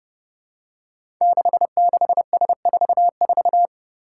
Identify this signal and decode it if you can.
Morse code, heard as a keyed tone, about 29 wpm, 700 Hz. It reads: 66S44
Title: Five Challenging Letter-Number Combinations 30wpm